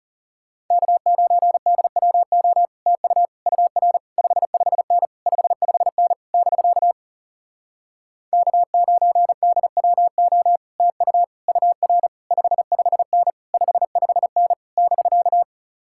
Morse code, heard as a keyed tone, about 37 wpm, 700 Hz. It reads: K9DWO TU UR 55N 55N <BK> K9DWO TU UR 55N 55N <BK>
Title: Speed-Racing Common QSO Phrases - 30wpm